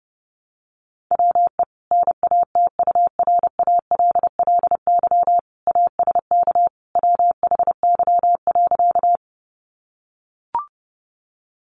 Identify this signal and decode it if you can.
Morse code, heard as a keyed tone, about 28 wpm, 700 Hz. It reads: WE NATURALLY ASK WHY.
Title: Sentences from Top 1300 Words 30wpm